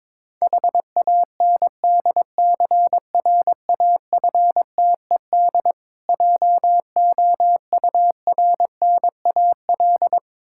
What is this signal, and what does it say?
Morse code, heard as a keyed tone, about 21 wpm, 690 Hz. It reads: HANDCRAFTED JOURNAL
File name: Difficult Words with Hints - 22wpm